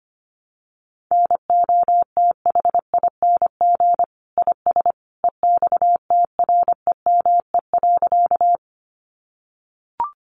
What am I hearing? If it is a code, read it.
Morse code, heard as a keyed tone, about 25 wpm, 700 Hz. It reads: NOTHING IS EXTREME.
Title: Sentences from Top 1300 Words 25wpm